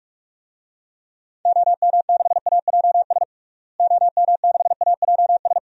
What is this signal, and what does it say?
Morse code, heard as a keyed tone, about 46 wpm, 700 Hz. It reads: OM6AJS OM6AJS
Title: DXCC-100 Call Signs - 6 Characters - Repeated 2x - 45wpm